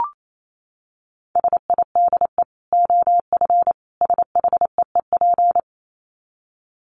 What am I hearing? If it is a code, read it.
Morse code, heard as a keyed tone, about 28 wpm, 700 Hz. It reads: SIDE OF SHEEP
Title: Dit Challenge 28wpm